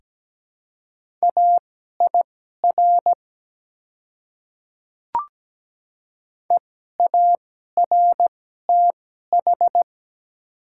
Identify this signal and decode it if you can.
Morse code, heard as a keyed tone, about 17 wpm, spaced as for 12 wpm, 700 Hz. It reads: AIR EARTH
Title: 2x Character Spacing - Top 200 Words - 17wpm